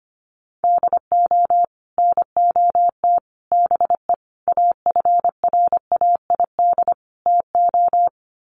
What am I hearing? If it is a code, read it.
Morse code, heard as a keyed tone, about 25 wpm, 700 Hz. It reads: DO NOT BE AFRAID TO
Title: Sets of 5 Words Encore2 25wpm